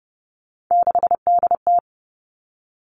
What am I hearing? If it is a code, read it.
Morse code, heard as a keyed tone, about 30 wpm, 700 Hz. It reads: 6DT
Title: Three Challenging Letter-Number Combinations 30wpm